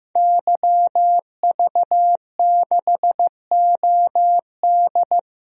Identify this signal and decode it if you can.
Morse code, heard as a keyed tone, about 15 wpm, 700 Hz. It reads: YV6OD
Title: DXCC-100 Call Signs - 5 Characters - 15wpm